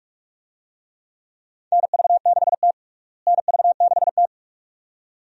Morse code, heard as a keyed tone, about 47 wpm, 700 Hz. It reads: NV6T NV6T
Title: USA Call Signs - 4 Characters - Repeated 2x - 45wpm